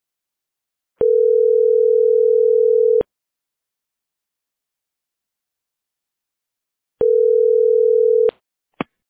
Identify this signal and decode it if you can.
Ringback tone (North America), 440 + 480 Hz.